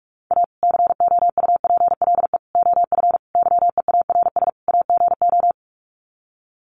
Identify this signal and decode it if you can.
Morse code, heard as a keyed tone, about 44 wpm, 700 Hz. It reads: A COUPLE OF YEARS AGO
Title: Sets of 5 Words 45wpm